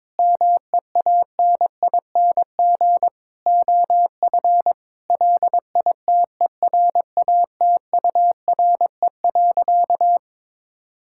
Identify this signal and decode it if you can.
Morse code, heard as a keyed tone, about 22 wpm, 700 Hz. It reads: MEANING OF LITERATURE.
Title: Sentences from Top 1000 Words - Review Part 6 - 22wpm